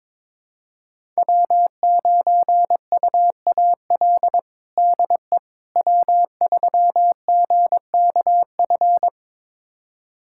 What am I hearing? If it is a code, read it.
Morse code, heard as a keyed tone, about 22 wpm, 700 Hz. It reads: W9UAL DE W3GKF
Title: USA Callsign DE Callsign - 5 Characters - 22wpm